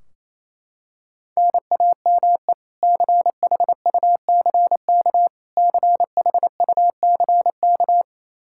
Morse code, heard as a keyed tone, about 28 wpm, 700 Hz. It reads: NAME CHUCK CHUCK
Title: K1USN SST QSO Practice 28wpm